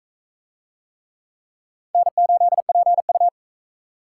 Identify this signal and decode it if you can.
Morse code, heard as a keyed tone, about 41 wpm, 700 Hz. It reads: N8PU